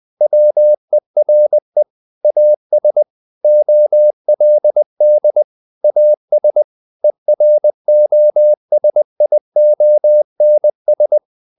Morse code, heard as a keyed tone, about 20 wpm, 600 Hz. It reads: WERE AS OLD AS EROSIONS
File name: the old man and the sea / the old man and the sea part 1 20wpm